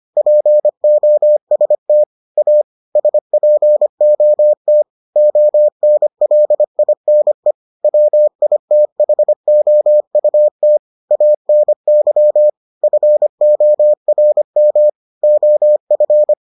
Morse code, heard as a keyed tone, about 25 wpm, 600 Hz. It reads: POST A SPOT ONLINE WITHOUT ANY FORM OF